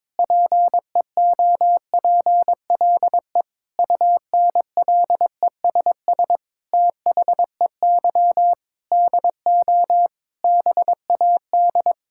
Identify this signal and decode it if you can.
Morse code, heard as a keyed tone, about 22 wpm, 700 Hz. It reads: PEOPLE UNLESS THEY DO BAD
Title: Sentences from Top 1000 Words - Review Part 6 - 22wpm